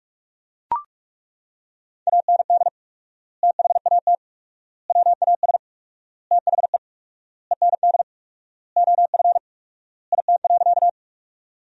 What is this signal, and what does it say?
Morse code, heard as a keyed tone, about 41 wpm, 700 Hz. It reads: AND THAT WAS THE END OF IT.
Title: 4x Word-Spacing - Sentences from Top 200 Words - 45wpm